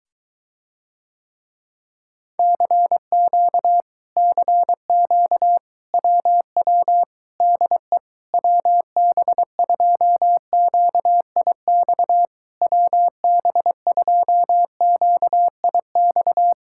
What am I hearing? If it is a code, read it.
Morse code, heard as a keyed tone, about 24 wpm, 700 Hz. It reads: CQ CQ WW DE WB2QIX WB2QIX